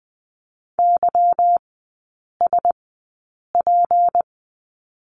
Morse code, heard as a keyed tone, about 20 wpm, spaced as for 8 wpm, 690 Hz. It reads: YSP
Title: 191 Lesson 31e - Letter v - Three Characters 20wpm